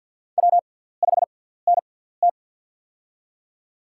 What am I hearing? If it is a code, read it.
Morse code, heard as a keyed tone, about 53 wpm, spaced as for 17 wpm, 700 Hz. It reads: W5NT
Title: USA Call Signs - 4 Characters - 6x Character Spacing - 50wpm